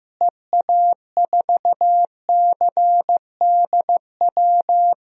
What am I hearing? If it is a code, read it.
Morse code, heard as a keyed tone, about 16 wpm, 700 Hz. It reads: EA4CDW
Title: DXCC-100 Call Signs - 6 Characters - Repeated 3x - 15wpm